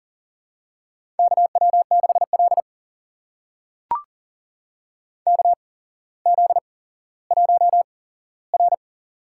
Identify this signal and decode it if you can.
Morse code, heard as a keyed tone, about 37 wpm, spaced as for 17 wpm, 700 Hz. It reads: KW6L KZ1R
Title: USA Call Signs - 4 Characters - 8x Character Spacing - 40wpm